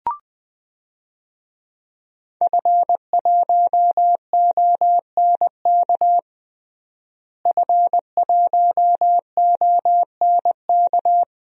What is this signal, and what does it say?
Morse code, heard as a keyed tone, about 21 wpm, 700 Hz. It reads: F1ONK F1ONK
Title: DXCC-100 Call Signs - 5 Characters - Repeated 2x - 20wpm